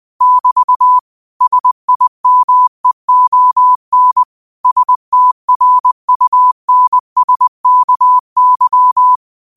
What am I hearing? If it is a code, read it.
Morse code, heard as a keyed tone, about 20 wpm, 1000 Hz. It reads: <BT> SIMEON STRUNSKY